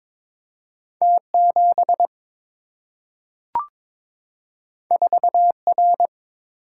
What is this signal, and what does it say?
Morse code, heard as a keyed tone, about 22 wpm, 700 Hz. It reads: T 7 4 R
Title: Single-Letter Single-Number 22wpm